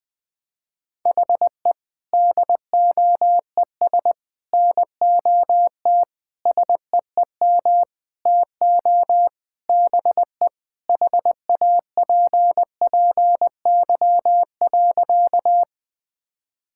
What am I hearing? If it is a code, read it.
Morse code, heard as a keyed tone, about 20 wpm, 700 Hz. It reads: HE DOES NOT SEEM TO BE HAPPY.
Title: Sentences with 5-Word Phrases - Spoken Beginning - 20wpm